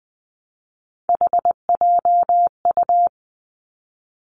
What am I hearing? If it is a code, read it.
Morse code, heard as a keyed tone, about 20 wpm, 700 Hz. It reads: HJU
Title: Three Challenging Letter-Number Combinations 20wpm